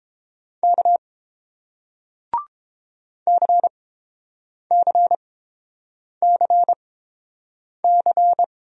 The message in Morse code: K CCCC

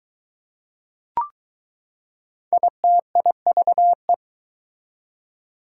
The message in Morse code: ITIVE

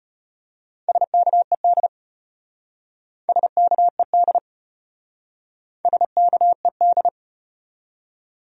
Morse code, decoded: SKED SKED SKED

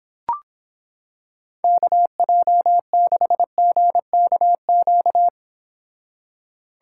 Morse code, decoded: KJ6GKQ

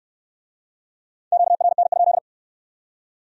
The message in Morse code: QRN?

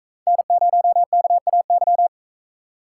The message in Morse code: N0KAY

